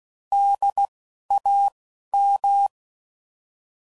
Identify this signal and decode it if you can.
Morse code, heard as a keyed tone, about 16 wpm, spaced as for 11 wpm, 800 Hz. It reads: DAM
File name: CW Code Groups @10WPM / fromSM's / Cities